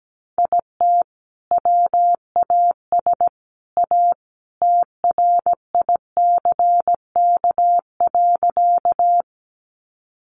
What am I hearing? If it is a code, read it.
Morse code, heard as a keyed tone, about 17 wpm, 700 Hz. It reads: IT WAS A TRICK.
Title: Sentences from Top 1500 Words 17wpm